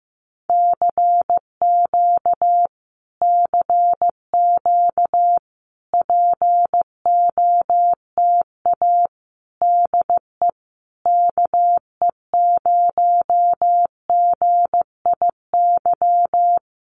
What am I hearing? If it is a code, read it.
Morse code, heard as a keyed tone, about 15 wpm, 700 Hz. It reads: CQ CQ POTA DE KE0GIY
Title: Common QSO Phrases Encore 15wpm